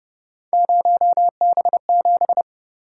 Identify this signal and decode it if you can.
Morse code, heard as a keyed tone, about 31 wpm, 700 Hz. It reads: 0B7
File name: Speed-Racing - Three Letter-Number Combinations - 25wpm